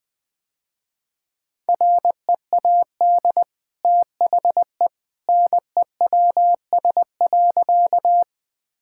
Morse code, heard as a keyed tone, about 20 wpm, 700 Hz. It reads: READ THE NEWS.
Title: Easy Sentences with Hints - 20wpm